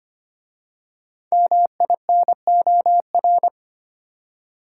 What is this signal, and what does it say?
Morse code, heard as a keyed tone, about 25 wpm, 700 Hz. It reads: MINOR
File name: 5 Letter Words - First 3 Letters Given - 25wpm